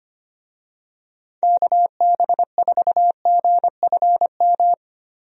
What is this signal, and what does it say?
Morse code, heard as a keyed tone, about 25 wpm, 700 Hz. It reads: KB4GFM
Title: USA Call Signs - 6 Characters - Repeated 2x - 25wpm